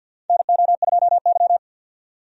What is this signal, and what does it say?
Morse code, heard as a keyed tone, about 50 wpm, 700 Hz. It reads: NO2Y
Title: USA Call Signs - 4 Characters - 50wpm